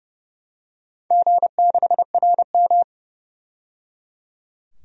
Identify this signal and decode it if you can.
Morse code, heard as a keyed tone, about 30 wpm, 700 Hz. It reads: G6RM
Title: DXCC-100 Call Signs - 4 Characters - 30wpm